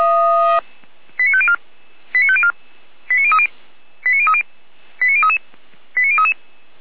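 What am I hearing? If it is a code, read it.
Five-tone selective call (ZVEI), heard as quick runs of five tones, tones of about 70 ms. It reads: 89504 89504 89030 89030 89033 89033
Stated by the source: ZVEI five-tone selective call for address 89042, received by SDR.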